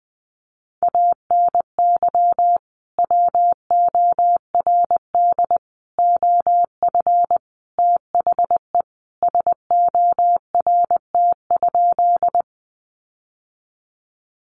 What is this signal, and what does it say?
Morse code, heard as a keyed tone, about 20 wpm, 700 Hz. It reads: ANY WORD OF THE SORT?
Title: Sentences from Top 1300 Words 20wpm